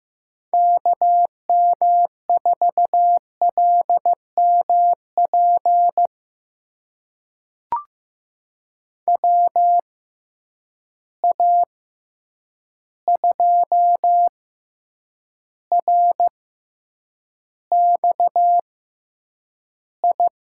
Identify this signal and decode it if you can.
Morse code, heard as a keyed tone, about 15 wpm, spaced as for 8 wpm, 700 Hz. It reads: KM4LMP WA2RXI
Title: USA Call Signs - 6 Characters - 6x Character Spacing - 15wpm